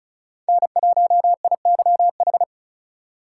Morse code, heard as a keyed tone, about 36 wpm, 700 Hz. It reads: N1IYH
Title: USA Call Signs - 3x Character Spacing - 35wpm